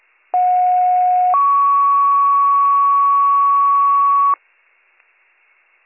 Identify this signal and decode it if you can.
Two-tone page, heard as a one-second tone, then a three-second tone, about 730 then 1120 Hz. It is heard from 0.3 to 4.4 s.